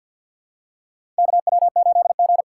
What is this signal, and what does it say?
Morse code, heard as a keyed tone, about 53 wpm, 700 Hz. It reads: KW8G
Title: USA Call Signs - 4 Characters - 6x Character Spacing - 50wpm